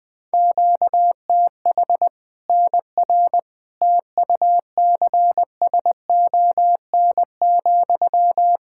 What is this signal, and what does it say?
Morse code, heard as a keyed tone, about 20 wpm, 700 Hz. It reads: QTH NR TUCSON,
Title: Practice QSO - Exchange 1 - Beginning - 20wpm